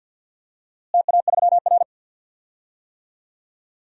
Morse code, heard as a keyed tone, about 52 wpm, 700 Hz. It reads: TA3R
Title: DXCC-100 Call Signs - 4 Characters - Repeated 3x - 50wpm